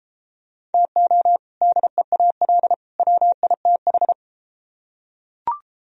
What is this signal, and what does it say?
Morse code, heard as a keyed tone, about 32 wpm, 700 Hz. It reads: TO DEAL WITH